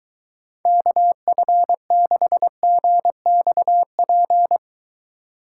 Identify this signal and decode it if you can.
Morse code, heard as a keyed tone, about 23 wpm, 700 Hz. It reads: KF6GXP